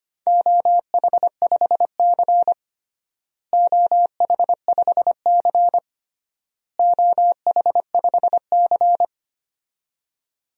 Morse code, heard as a keyed tone, about 25 wpm, 700 Hz. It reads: OH5C OH5C OH5C
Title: DXCC-100 Call Signs - 4 Characters - Repeated 3x - 25wpm